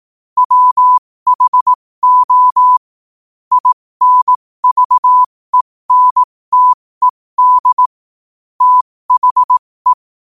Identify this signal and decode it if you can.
Morse code, heard as a keyed tone, about 18 wpm, spaced as for 15 wpm, 1000 Hz. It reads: WHO INVENTED THE
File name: qotdfiles / 15wpm